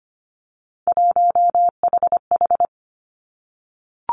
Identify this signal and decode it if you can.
Morse code, heard as a keyed tone, about 25 wpm, 700 Hz. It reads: 1HH
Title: Three Challenging Letter-Number Combinations 25wpm